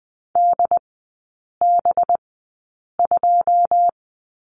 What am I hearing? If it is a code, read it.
Morse code, heard as a keyed tone, about 20 wpm, spaced as for 8 wpm, 700 Hz. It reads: DB2
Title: 184 Lesson 30e - Letter b - Three Characters 20wpm